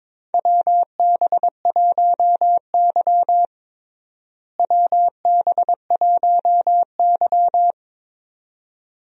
Morse code, heard as a keyed tone, about 22 wpm, 700 Hz. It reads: WB1Y WB1Y
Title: USA Call Signs - 4 Characters - Repeated 2x - 22wpm